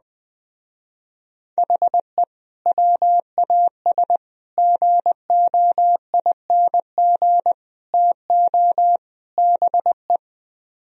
Morse code, heard as a keyed tone, about 20 wpm, 700 Hz. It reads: HE WAS GOING TO BE
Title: Speed-Racing - Sets of 5 Words 17wpm